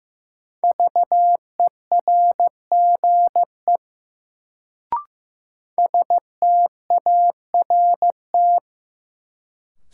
Morse code, heard as a keyed tone, about 15 wpm, 700 Hz. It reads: VERGE START
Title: Sets of 4 Words - New Word Drill - 15wpm